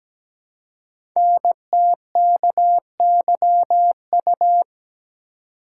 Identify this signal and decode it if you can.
Morse code, heard as a keyed tone, about 17 wpm, 700 Hz. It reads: NTKYU
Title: Code Groups - 5 Letters - 17wpm